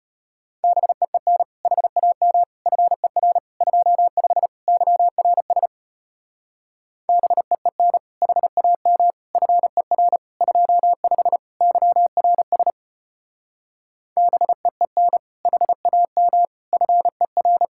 Morse code, heard as a keyed tone, about 33 wpm, 700 Hz. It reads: BEEN HAM FER 25 YRS BEEN HAM FER 25 YRS BEEN HAM FER